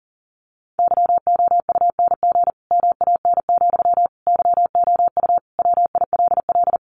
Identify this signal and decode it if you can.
Morse code, heard as a keyed tone, about 40 wpm, 700 Hz. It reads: YOUNG MAN, YOU WILL
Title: Sentences from Top 400 Words 40wpm